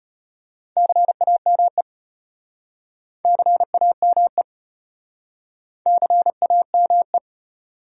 Morse code, heard as a keyed tone, about 35 wpm, 700 Hz. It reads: CAME CAME CAME